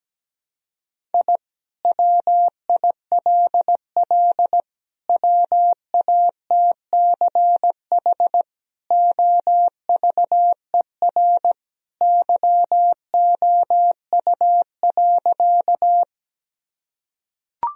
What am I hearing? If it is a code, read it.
Morse code, heard as a keyed tone, about 17 wpm, 700 Hz. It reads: I WILL WATCH OVER YOU.
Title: Sentences from Top 1000 Words - Review Part 1 - 17wpm